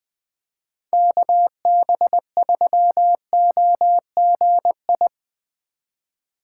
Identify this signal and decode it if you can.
Morse code, heard as a keyed tone, about 20 wpm, 700 Hz. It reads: KB3OGI
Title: USA Call Signs - 6 Characters - 2x Character Spacing - 20wpm